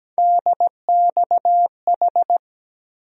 Morse code, heard as a keyed tone, about 17 wpm, 700 Hz. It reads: DXH
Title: USA Call Sign Postfixes - 3 Letters 17wpm